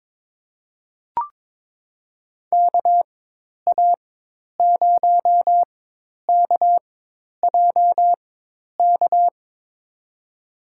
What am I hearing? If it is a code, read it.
Morse code, heard as a keyed tone, about 21 wpm, spaced as for 10 wpm, 700 Hz. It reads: KA0KJK
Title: USA Call Signs - 6 Characters - 4x Character Spacing - 22wpm